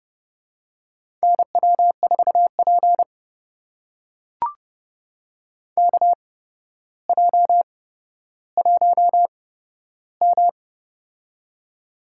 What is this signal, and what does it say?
Morse code, heard as a keyed tone, about 30 wpm, spaced as for 13 wpm, 700 Hz. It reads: NW4P KJ1M
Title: USA Call Signs - 4 Characters - 8x Character Spacing - 30wpm